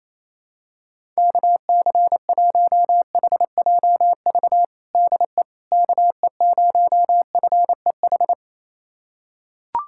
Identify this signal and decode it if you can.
Morse code, heard as a keyed tone, about 28 wpm, 700 Hz. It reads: KC1HJV DE KE0FEH
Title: USA Callsign DE Callsign 28wpm